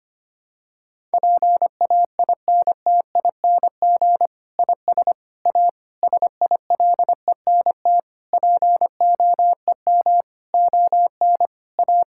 Morse code, heard as a keyed tone, about 25 wpm, 700 Hz. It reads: PAINTING IS A SILENT POEM ON A
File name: practice / 25wpm / Difficult Sentences with Hints - 25wpm